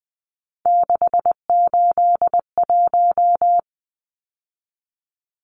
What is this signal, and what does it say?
Morse code, heard as a keyed tone, about 20 wpm, 700 Hz. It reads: 681
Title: Numbers - 3-Digit 20wpm